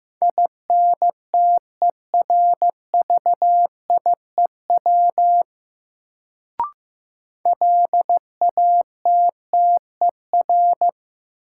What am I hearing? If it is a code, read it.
Morse code, heard as a keyed tone, about 15 wpm, 700 Hz. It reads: INTERVIEW LATTER
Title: Sentences from Top 1600 Words - New Word Drill Alt - 15wpm